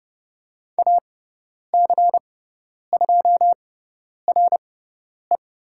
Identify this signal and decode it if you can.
Morse code, heard as a keyed tone, about 30 wpm, spaced as for 10 wpm, 690 Hz. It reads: AC2RE
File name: Lesson 18g - Letter m - Call Signs 30wpm